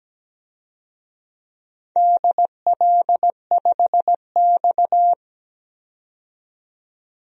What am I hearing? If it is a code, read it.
Morse code, heard as a keyed tone, about 17 wpm, 700 Hz. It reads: DL5X
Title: DXCC-100 Call Signs - 4 Characters - 17wpm